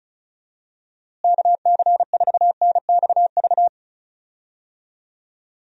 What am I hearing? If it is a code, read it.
Morse code, heard as a keyed tone, about 36 wpm, 700 Hz. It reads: KC4NXV
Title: USA Call Signs - 6 Characters - Repeated 3x - 35wpm